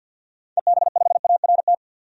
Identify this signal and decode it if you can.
Morse code, heard as a keyed tone, about 45 wpm, 700 Hz. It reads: EB5ART
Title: DXCC-100 Call Signs - 6 Characters - 50wpm